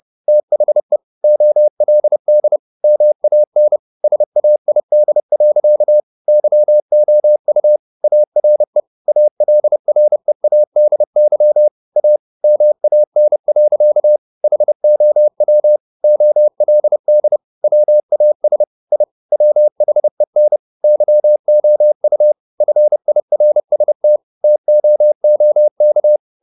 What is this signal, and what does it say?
Morse code, heard as a keyed tone, about 30 wpm, 600 Hz. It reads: THE OLD MAN SAID. YOU ARE ALREADY A MAN. HOW OLD WAS I WHEN YOU FIRST TOOK
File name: the old man and the sea / the old man and the sea 1 30wpm